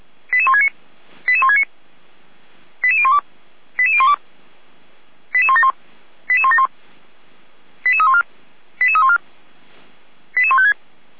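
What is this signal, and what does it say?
Five-tone selective call (ZVEI), heard as quick runs of five tones, tones of about 70 ms. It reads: 88169 88169 88012 88012 88171 88171 88314 88314 88156
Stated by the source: SDR-received ZVEI five-tone selective call.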